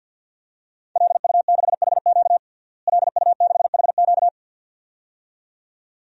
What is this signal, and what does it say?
Morse code, heard as a keyed tone, about 51 wpm, 700 Hz. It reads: LU6HQ LU6HQ
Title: DXCC-100 Call Signs - 5 Characters - Repeated 2x - 50wpm